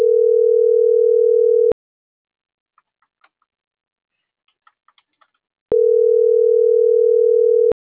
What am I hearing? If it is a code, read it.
Ringback tone (North America), 440 + 480 Hz.
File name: CENTRAL MABE / out mariel.reyes